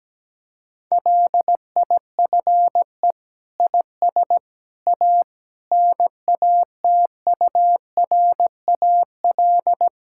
Morse code, heard as a keyed tone, about 17 wpm, 700 Hz. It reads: LIFE IS A NATURAL
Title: Sentences from Top 1000 Words - Review Part 6 - 17wpm